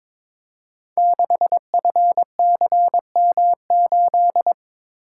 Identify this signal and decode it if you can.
Morse code, heard as a keyed tone, about 22 wpm, 700 Hz. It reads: 6FCM8
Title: Code Groups - 5 Letter:Numbers - 22wpm